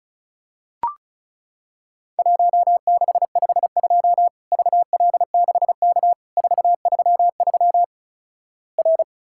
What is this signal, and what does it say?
Morse code, heard as a keyed tone, about 36 wpm, 700 Hz. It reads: 1652 VL6K 433 R
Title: Worked All Europe DX Contest - QTC Exchange - 35wpm